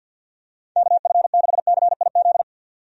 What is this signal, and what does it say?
Morse code, heard as a keyed tone, about 51 wpm, 700 Hz. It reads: KF6CIZ
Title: USA Call Signs - 6 Characters - 50wpm